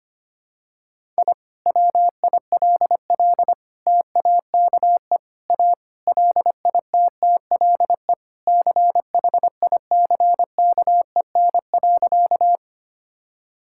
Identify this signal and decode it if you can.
Morse code, heard as a keyed tone, about 25 wpm, 700 Hz. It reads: I WILL TAKE A LITTLE CHICKEN.
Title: Sentences from Top 1000 Words - Review Part 5 - 25wpm